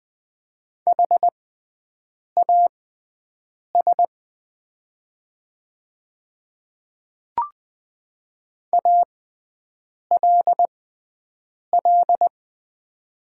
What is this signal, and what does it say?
Morse code, heard as a keyed tone, about 20 wpm, spaced as for 7 wpm, 690 Hz. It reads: HAS ALL
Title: 6x Character Spacing - Top 100 Words - 20wpm